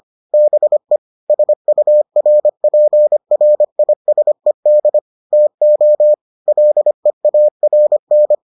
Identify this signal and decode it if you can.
Morse code, heard as a keyed tone, about 25 wpm, 600 Hz. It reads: BE SURPRISED TO LEARN